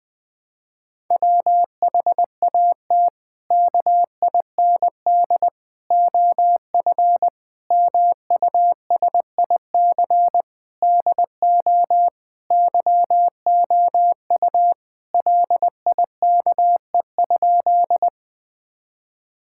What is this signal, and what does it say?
Morse code, heard as a keyed tone, about 20 wpm, 700 Hz. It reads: WHAT KIND OF MUSIC DO YOU LIKE?
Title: Sentences with 3-Word Phrases - 20wpm